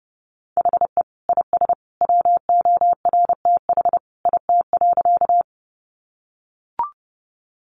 Morse code, heard as a keyed tone, about 30 wpm, 700 Hz. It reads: HE IS WORTH IT.
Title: Sentences from Top 800 Words 30wpm